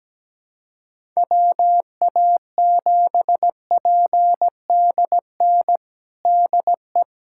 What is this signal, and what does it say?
Morse code, heard as a keyed tone, about 17 wpm, 700 Hz. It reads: WA7PDN DE
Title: USA Callsign DE Callsign 17wpm